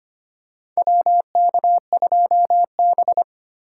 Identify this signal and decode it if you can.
Morse code, heard as a keyed tone, about 25 wpm, 700 Hz. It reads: WK2B
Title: USA Call Signs - 8x Character Spacing - 25wpm